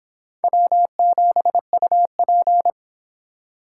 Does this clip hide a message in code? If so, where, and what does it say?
Yes: Morse code from 0.4 to 2.7 s, heard as a keyed tone, about 30 wpm, 700 Hz. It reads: W7UP